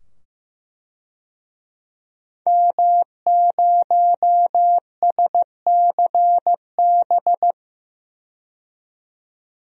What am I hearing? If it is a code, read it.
Morse code, heard as a keyed tone, about 15 wpm, 700 Hz. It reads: M0SCB
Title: DXCC-100 Call Signs - 5 Characters - 15wpm